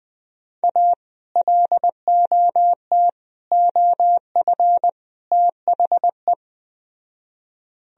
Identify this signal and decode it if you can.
Morse code, heard as a keyed tone, about 20 wpm, 700 Hz. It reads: A LOT OF THE
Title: Speed-Racing - Sets of 4 Words 17wpm